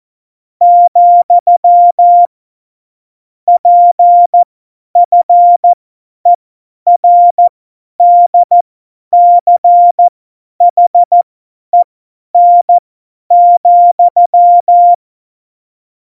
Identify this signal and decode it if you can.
Morse code, heard as a keyed tone, about 14 wpm, spaced as for 10 wpm, 700 Hz. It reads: , PFERDCHEN,
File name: Morsekurs / Witze / Witz 19. Januar 2021, 50 F70